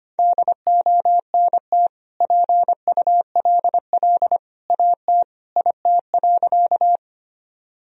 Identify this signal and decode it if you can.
Morse code, heard as a keyed tone, about 25 wpm, 700 Hz. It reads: DONT PULL AT IT.
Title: Sentences from Top 1000 Words - Review Part 2 - 25wpm